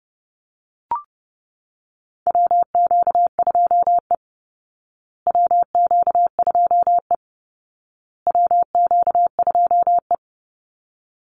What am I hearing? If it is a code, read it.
Morse code, heard as a keyed tone, about 30 wpm, 700 Hz. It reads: WQ2E WQ2E WQ2E
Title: Call Signs Repeated 2x - 30wpm